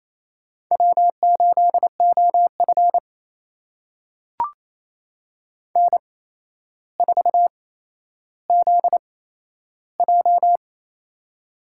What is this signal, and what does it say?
Morse code, heard as a keyed tone, about 28 wpm, spaced as for 12 wpm, 700 Hz. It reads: W8OF N4ZJ
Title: USA Call Signs - 4 Characters - 8x Character Spacing - 28wpm